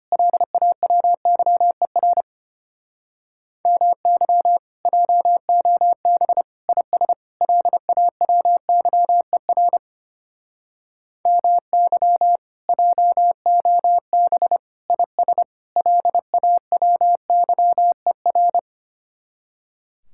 Morse code, heard as a keyed tone, about 32 wpm, 700 Hz. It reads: LAWYER MY JOB IS LAWYER MY JOB IS LAWYER